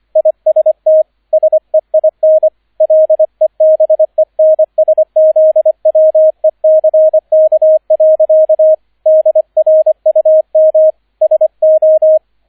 Morse code, heard as a keyed tone, about 25 wpm, 630 Hz, with radio noise. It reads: IST SEIN LEBENSZWECK. DRUM SO